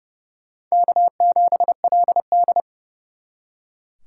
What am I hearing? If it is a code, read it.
Morse code, heard as a keyed tone, about 36 wpm, 700 Hz. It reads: K7LD
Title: Speed Racing - USA 1x2 Call Signs 30wpm